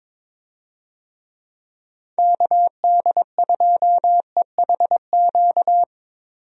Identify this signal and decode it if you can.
Morse code, heard as a keyed tone, about 22 wpm, 700 Hz. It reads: KD2EHQ
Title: USA Call Signs - 6 Characters - 22wpm